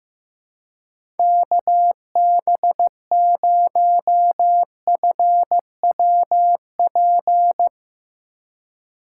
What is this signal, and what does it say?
Morse code, heard as a keyed tone, about 15 wpm, 700 Hz. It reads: KB0FWP
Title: USA Call Signs - 6 Characters - 3x Character Spacing - 15wpm